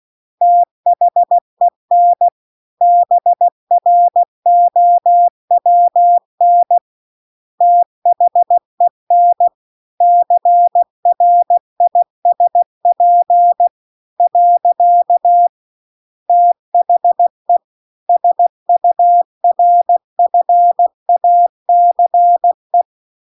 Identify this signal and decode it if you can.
Morse code, heard as a keyed tone, about 16 wpm, 700 Hz. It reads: THEN BROWN THEN CRISP . THE SURFACE